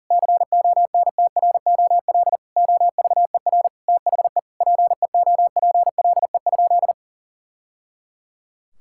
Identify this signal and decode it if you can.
Morse code, heard as a keyed tone, about 37 wpm, 700 Hz. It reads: CONTROL OVER THE PEOPLE?
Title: Sentences from Top 1000 Words - Review Part 5 - 40wpm